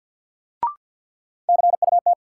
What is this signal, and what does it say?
Morse code, heard as a keyed tone, about 48 wpm, 700 Hz. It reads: CUT
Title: Sentences from Top 300 Words - New Word Drill - 50wpm